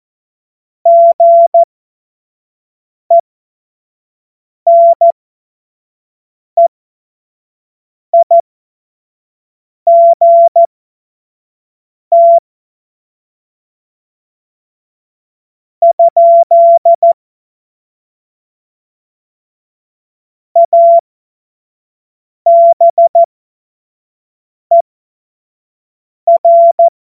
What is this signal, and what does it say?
Morse code, heard as a keyed tone, about 14 wpm, spaced as for 5 wpm, 680 Hz. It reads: GENEIGT? ABER
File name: Morsekurs